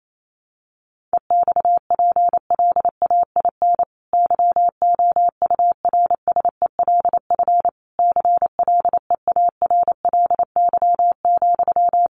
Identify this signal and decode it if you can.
Morse code, heard as a keyed tone, about 28 wpm, 700 Hz. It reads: EXPLAIN YOURSELF CLEARLY,
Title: Sentences from Top 1300 Words 28wpm